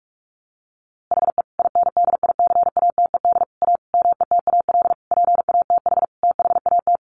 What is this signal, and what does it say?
Morse code, heard as a keyed tone, about 44 wpm, 700 Hz. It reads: HE INDICATED A METAL PATH THAT
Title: A Sound Of Thunder 45wpm